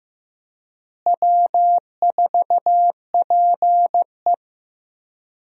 Morse code, heard as a keyed tone, about 15 wpm, 700 Hz. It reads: W4PE
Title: USA Call Signs - 4 Characters - Repeated 3x - 15wpm